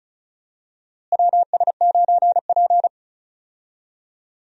Morse code, heard as a keyed tone, about 36 wpm, 700 Hz. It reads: WS9P
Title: USA Call Signs - 4 Characters - Repeated 2x - 35wpm